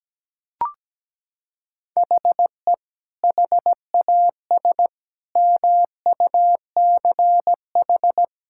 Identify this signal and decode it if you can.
Morse code, heard as a keyed tone, about 17 wpm, 700 Hz. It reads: HE HAS MUCH
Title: Sentences from Top 1000 Words - Review Part 5 - 17wpm